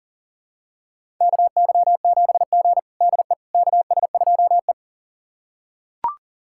Morse code, heard as a keyed tone, about 40 wpm, 700 Hz. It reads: KY7G DE KS2E
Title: USA Callsign DE Callsign - 4 Characters - 40wpm